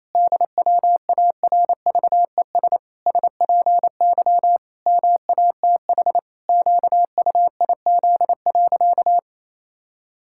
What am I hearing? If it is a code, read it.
Morse code, heard as a keyed tone, about 28 wpm, 700 Hz. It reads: DWARVES SPY MATH QUIZ.
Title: Pangram Challenge v2 - 28wpm